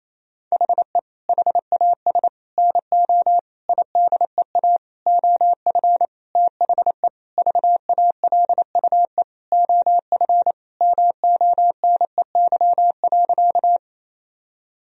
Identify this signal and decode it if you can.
Morse code, heard as a keyed tone, about 28 wpm, 700 Hz. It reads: HE HAS NO IDEA OF THE VALUE OF MONEY.
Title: Sentences from Top 1000 Words - Review Part 3 - 28wpm